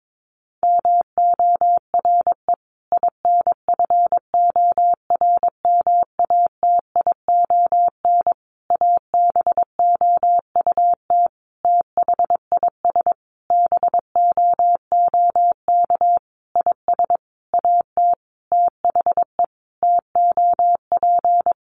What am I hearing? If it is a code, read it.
Morse code, heard as a keyed tone, about 22 wpm, 700 Hz. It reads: MORE INFORMATION ABOUT THIS BOOK IS AT THE TOP
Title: Sentences from Top 1200 Words 22wpm